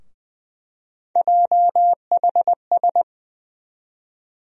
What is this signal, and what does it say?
Morse code, heard as a keyed tone, about 20 wpm, 700 Hz. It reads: JHS
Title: USA Call Sign Postfixes - 3 Letters 20wpm